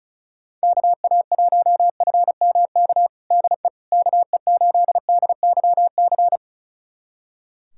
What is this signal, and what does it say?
Morse code, heard as a keyed tone, about 36 wpm, 690 Hz. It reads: KA1FMK DE KE8DYC
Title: USA Callsign DE Callsign - 6 Characters - 35wpm